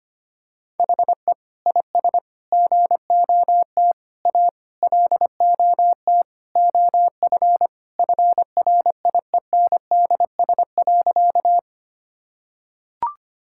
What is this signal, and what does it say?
Morse code, heard as a keyed tone, about 25 wpm, 700 Hz. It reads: HE IS GOT A LOT OF FRIENDS.
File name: practice / 25wpm / Sentences with 4-Word Phrases - 25wpm